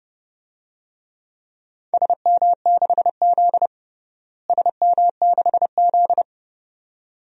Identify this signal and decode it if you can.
Morse code, heard as a keyed tone, about 30 wpm, 700 Hz. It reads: SM6Z SM6Z
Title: DXCC-100 Call Signs - 4 Characters - Repeated 2x - 30wpm